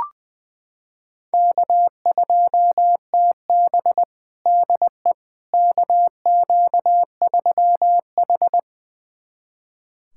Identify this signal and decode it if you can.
Morse code, heard as a keyed tone, about 21 wpm, 700 Hz. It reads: K2TB DE KQ3H
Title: USA Callsign DE Callsign - 4 Characters - 20wpm